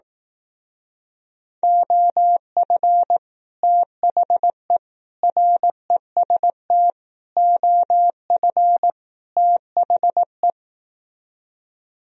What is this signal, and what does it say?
Morse code, heard as a keyed tone, about 18 wpm, 700 Hz. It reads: OF THE REST OF THE
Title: Speed-Racing - Sets of 5 Words 15wpm